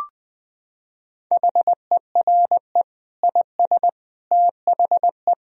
Morse code, heard as a keyed tone, about 20 wpm, 700 Hz. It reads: HERE IS THE
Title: Sentences from Top 1000 Words - Review Part 2 - 20wpm